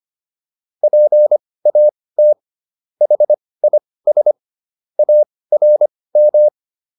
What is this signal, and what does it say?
Morse code, heard as a keyed tone, about 25 wpm, spaced as for 18 wpm, 600 Hz. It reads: PAT HIS ARM